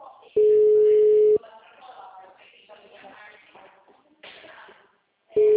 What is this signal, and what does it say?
Ringback tone (Europe, 425 Hz), with speech.
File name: out- -2004- - - .17036